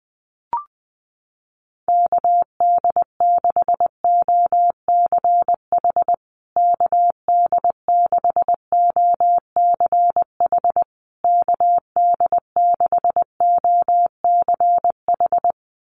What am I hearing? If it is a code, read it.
Morse code, heard as a keyed tone, about 20 wpm, 700 Hz. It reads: KD6OCH KD6OCH KD6OCH
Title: USA Call Signs Repeated 3x 20wpm